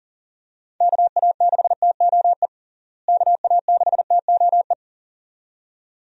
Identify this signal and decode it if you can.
Morse code, heard as a keyed tone, about 37 wpm, 700 Hz. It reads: KA6TOE KA6TOE
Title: USA Call Signs - 6 Characters - Repeated 2x - 40wpm